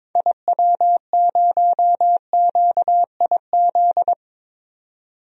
Morse code, heard as a keyed tone, about 22 wpm, 700 Hz. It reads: IW0QIZ
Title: DXCC-100 Call Signs - 6 Characters - Repeated 3x - 22wpm